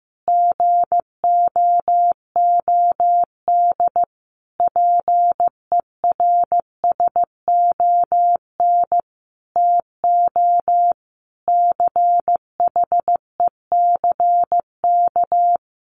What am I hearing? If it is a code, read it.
Morse code, heard as a keyed tone, about 14 wpm, 700 Hz. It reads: GOOD PERSON TO CHECK
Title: Sentences from Top 1400 Words 15wpm